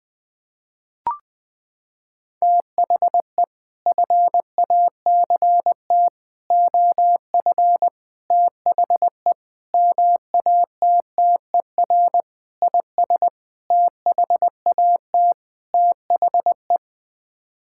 Morse code, heard as a keyed tone, about 21 wpm, 700 Hz. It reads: THE FACT OF THE MATTER IS THAT THE
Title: Sets of 8 Words - 20wpm